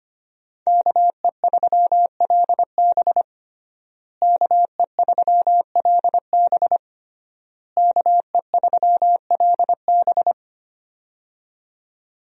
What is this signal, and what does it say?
Morse code, heard as a keyed tone, about 25 wpm, 700 Hz. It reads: KE3LB KE3LB KE3LB
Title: USA Call Signs - 5 Characters - Repeated 3x - 25wpm